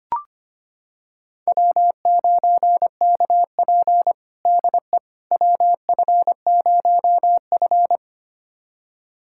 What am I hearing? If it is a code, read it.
Morse code, heard as a keyed tone, about 25 wpm, 700 Hz. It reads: W9KP DE WF0F
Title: USA Callsign DE Callsign - 4 Characters - 25wpm